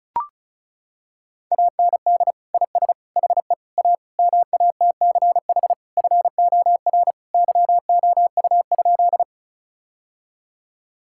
Morse code, heard as a keyed tone, about 34 wpm, 700 Hz. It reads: AND IS HE A MATCH FOR YOU?
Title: Sentences from Top 1600 Words 35wpm